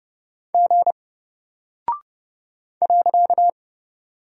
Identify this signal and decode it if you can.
Morse code, heard as a keyed tone, about 30 wpm, 700 Hz. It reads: G .
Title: Single Character (Letter:Number:Punctuation:Prosign) Alt 30wpm